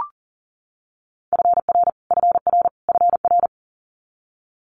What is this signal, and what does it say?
Morse code, heard as a keyed tone, about 40 wpm, 700 Hz. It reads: FR FR FR
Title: Multi-letter Phonemes Repeated 3x 40wpm